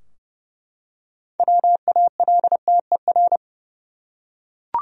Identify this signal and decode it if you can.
Morse code, heard as a keyed tone, about 30 wpm, 700 Hz. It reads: WALTER
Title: USA Common Names - 30wpm